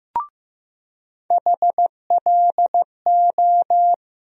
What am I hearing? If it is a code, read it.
Morse code, heard as a keyed tone, about 15 wpm, 700 Hz. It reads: HLO
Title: USA Call Sign Postfixes - 3 Letters 15wpm